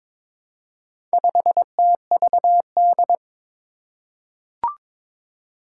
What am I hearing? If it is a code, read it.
Morse code, heard as a keyed tone, about 22 wpm, 700 Hz. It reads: 5TVD
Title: Four Challenging Letter-Number Combinations 22wpm